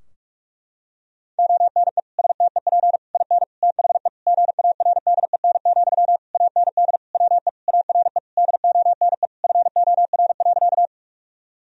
Morse code, heard as a keyed tone, about 46 wpm, 700 Hz. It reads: ONE STEP IN THE GARDEN, AND WE ARE DONE FOR.
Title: Sentences from Top 1000 Words - Review Part 4 - 45wpm